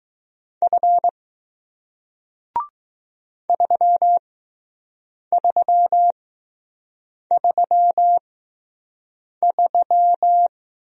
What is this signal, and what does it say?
Morse code, heard as a keyed tone, about 20 wpm, spaced as for 6 wpm, 700 Hz. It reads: F 3333